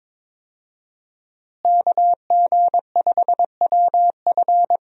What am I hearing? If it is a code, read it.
Morse code, heard as a keyed tone, about 21 wpm, 700 Hz. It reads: KG5WF
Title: USA Call Signs - 5 Characters - Repeated 3x - 22wpm